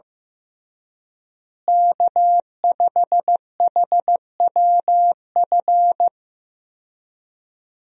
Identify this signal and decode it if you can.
Morse code, heard as a keyed tone, about 15 wpm, 700 Hz. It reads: K5HWF
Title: USA Call Signs - 5 Characters - Repeated 3x - 15wpm